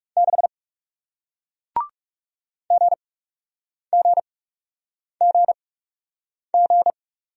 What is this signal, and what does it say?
Morse code, heard as a keyed tone, about 40 wpm, spaced as for 8 wpm, 700 Hz. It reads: 6 GGGG